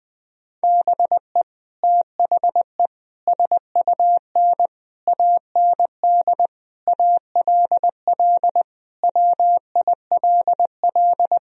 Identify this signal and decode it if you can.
Morse code, heard as a keyed tone, about 21 wpm, 700 Hz. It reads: BE THE SUN AND ALL WILL
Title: Sentences from Top 1000 Words - Review Part 1 - 20wpm